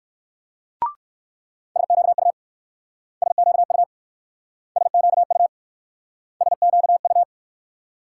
Morse code, heard as a keyed tone, about 63 wpm, 690 Hz. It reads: SQU SQU SQU SQU